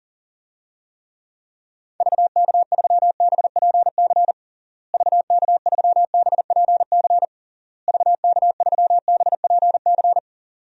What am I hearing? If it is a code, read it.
Morse code, heard as a keyed tone, about 40 wpm, 700 Hz. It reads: VK3BPC VK3BPC VK3BPC
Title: DXCC-100 Call Signs - 6 Characters - Repeated 3x - 40wpm